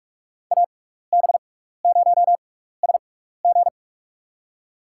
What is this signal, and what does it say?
Morse code, heard as a keyed tone, about 47 wpm, spaced as for 16 wpm, 700 Hz. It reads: AB0SG
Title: USA Call Signs - 5 Characters - 6x Character Spacing - 45wpm